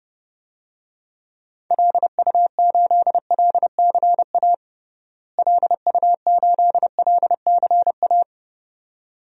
Morse code, heard as a keyed tone, about 30 wpm, 700 Hz. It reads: LU8LCA LU8LCA
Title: DXCC-100 Call Signs - 6 Characters - Repeated 2x - 30wpm